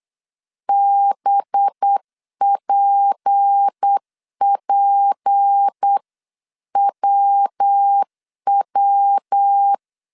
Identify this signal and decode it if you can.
Morse code, heard as a keyed tone, about 8 wpm, 800 Hz. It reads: BPPWW
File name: CW / DET / 1 6 1500C S B 8wpm 5142